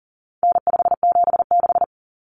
Morse code, heard as a keyed tone, about 40 wpm, 700 Hz. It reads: N576
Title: Common Cut Numbers - 4-Digit 40wpm